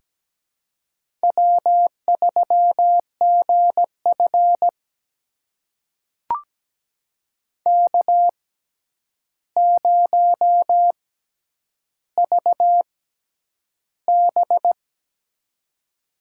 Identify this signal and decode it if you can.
Morse code, heard as a keyed tone, about 17 wpm, spaced as for 9 wpm, 700 Hz. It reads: W3GF K0VB
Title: USA Call Signs - 4 Characters - 6x Character Spacing - 17wpm